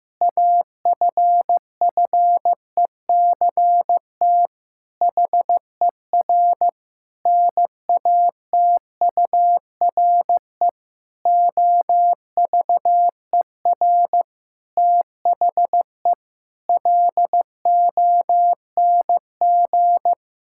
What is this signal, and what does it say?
Morse code, heard as a keyed tone, about 15 wpm, 700 Hz. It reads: AFFECT HER NATURE OVER THE LONG
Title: Sentences from Top 1000 Words - Review Part 6 - 15wpm